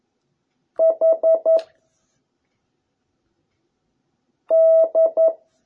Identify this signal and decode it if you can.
Morse code, heard as a keyed tone, about 11 wpm, 660 Hz. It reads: H D